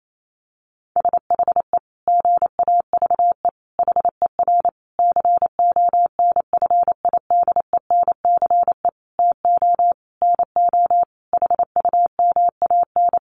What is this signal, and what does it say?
Morse code, heard as a keyed tone, about 28 wpm, 700 Hz. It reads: SHE GAVE HER CONFIDENCE TO NO HUMAN
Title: Sentences from Top 1500 Words 28wpm